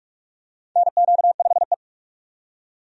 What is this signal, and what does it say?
Morse code, heard as a keyed tone, about 45 wpm, 700 Hz. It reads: NQ5E
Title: USA Call Signs - 4 Characters - 45wpm